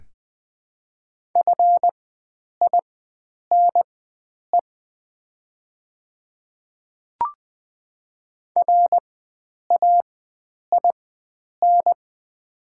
Morse code, heard as a keyed tone, about 20 wpm, spaced as for 9 wpm, 700 Hz. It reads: FINE RAIN
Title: 4x Character Spacing - Top 500 Words - 20wpm